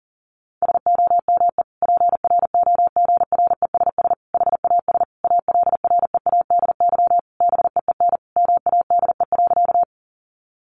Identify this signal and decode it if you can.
Morse code, heard as a keyed tone, about 40 wpm, 700 Hz. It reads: SOME PROGRESS HAS ALREADY BEEN MADE.
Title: Sentences from Top 1400 Words 40wpm